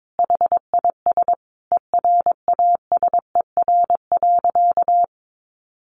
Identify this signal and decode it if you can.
Morse code, heard as a keyed tone, about 22 wpm, 700 Hz. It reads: HIS ERASER.
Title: Additional Easy Phrases 22wpm